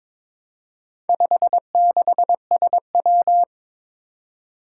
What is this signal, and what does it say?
Morse code, heard as a keyed tone, about 22 wpm, 700 Hz. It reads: 56SW
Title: Four Challenging Letter-Number Combinations 22wpm